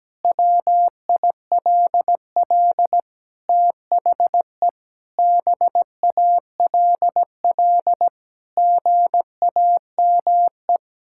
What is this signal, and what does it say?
Morse code, heard as a keyed tone, about 17 wpm, 700 Hz. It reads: WILL THE BALL GAME
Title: Sentences from Top 1000 Words - Review Part 6 - 17wpm